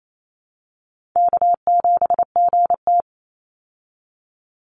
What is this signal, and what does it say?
Morse code, heard as a keyed tone, about 28 wpm, 700 Hz. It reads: K7GT
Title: Call Signs Repeated 2x - 28wpm